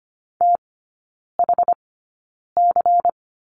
Morse code, heard as a keyed tone, about 25 wpm, spaced as for 9 wpm, 700 Hz. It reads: THC
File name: course / 25:10wpm / 093 Lesson 16e - Letter c - Three Characters 25wpm